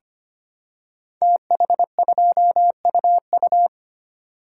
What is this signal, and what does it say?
Morse code, heard as a keyed tone, about 25 wpm, 700 Hz. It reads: TH2UU
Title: Code Groups - 5 Letter:Numbers - 25wpm